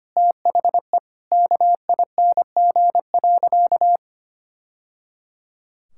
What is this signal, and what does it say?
Morse code, heard as a keyed tone, about 25 wpm, 700 Hz. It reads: THE KING.
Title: Sentences from Top 1000 Words - Review Part 5 - 25wpm